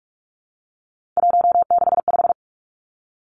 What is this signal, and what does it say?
Morse code, heard as a keyed tone, about 46 wpm, 700 Hz. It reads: 165
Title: Three Challenging Letter-Number Combinations 45wpm